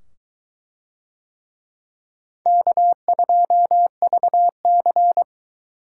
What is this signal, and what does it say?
Morse code, heard as a keyed tone, about 22 wpm, 700 Hz. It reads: K2VC